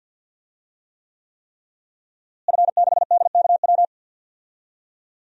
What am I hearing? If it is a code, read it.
Morse code, heard as a keyed tone, about 51 wpm, 700 Hz. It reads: F6DKW
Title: DXCC-100 Call Signs - 5 Characters - Repeated 3x - 50wpm